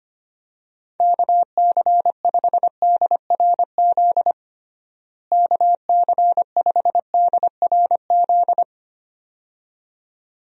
Morse code, heard as a keyed tone, about 25 wpm, 700 Hz. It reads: KC5DRZ KC5DRZ
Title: USA Call Signs - 6 Characters - Repeated 2x - 25wpm